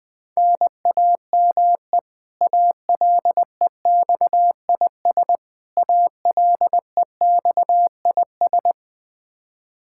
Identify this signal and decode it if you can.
Morse code, heard as a keyed tone, about 20 wpm, 700 Hz. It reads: NAME ALEXIS ALEXIS
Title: K1USN SST QSO Practice 20wpm